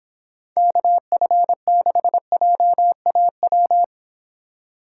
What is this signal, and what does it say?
Morse code, heard as a keyed tone, about 30 wpm, 700 Hz. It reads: KF6JAW